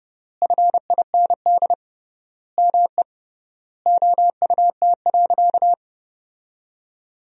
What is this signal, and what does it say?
Morse code, heard as a keyed tone, about 30 wpm, 700 Hz. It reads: FIND ME OUT.
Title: 3x Word-Spacing - Sentences from Top 500 Words - 30wpm